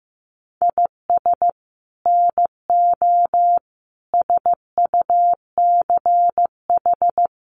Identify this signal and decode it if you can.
Morse code, heard as a keyed tone, about 15 wpm, 700 Hz. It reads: IS NO SUCH
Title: Sets of 5 Words Encore 15wpm